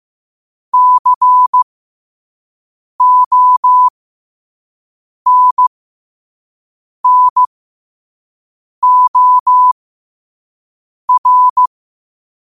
Morse code, heard as a keyed tone, about 15 wpm, spaced as for 5 wpm, 1000 Hz. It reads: CONNOR